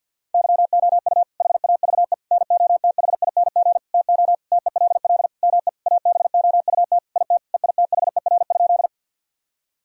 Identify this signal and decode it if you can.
Morse code, heard as a keyed tone, about 50 wpm, 700 Hz. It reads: YOU HAVE NOTHING TO TELL ME ABOUT IT EITHER?
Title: Sentences from Top 1000 Words - Review Part 4 - 50wpm